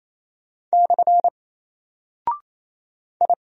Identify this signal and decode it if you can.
Morse code, heard as a keyed tone, about 27 wpm, 700 Hz. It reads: / I
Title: Single Character (Letter:Number:Punctuation:Prosign) 28wpm